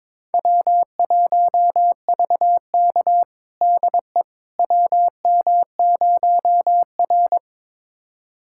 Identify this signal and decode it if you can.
Morse code, heard as a keyed tone, about 22 wpm, 700 Hz. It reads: W1VK DE WM0R
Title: USA Callsign DE Callsign - 4 Characters - 22wpm